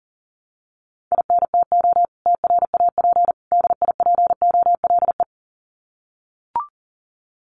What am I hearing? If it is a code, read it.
Morse code, heard as a keyed tone, about 40 wpm, 700 Hz. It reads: INTO TRAP DIPOLE
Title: Common QSO Phrases 40wpm